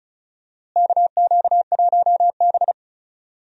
Morse code, heard as a keyed tone, about 36 wpm, 700 Hz. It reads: KQ1B
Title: USA Call Signs - 4 Characters - Repeated 2x - 35wpm